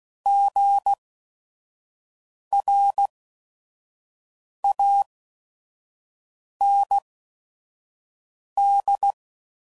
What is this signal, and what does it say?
Morse code, heard as a keyed tone, about 16 wpm, spaced as for 5 wpm, 800 Hz. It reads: GRAND
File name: Prefixes / By First Letter / Gs